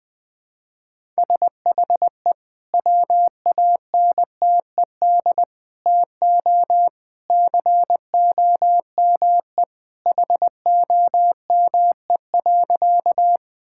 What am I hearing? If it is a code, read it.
Morse code, heard as a keyed tone, about 20 wpm, 700 Hz. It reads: SHE WANTED TO COME HOME.
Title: Sentences with 2-Word Phrases - Spoken Beginning - 20wpm